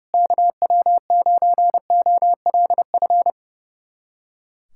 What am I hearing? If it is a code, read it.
Morse code, heard as a keyed tone, about 32 wpm, 690 Hz. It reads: KW9OLF
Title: USA Callsign DE Callsign 30wpm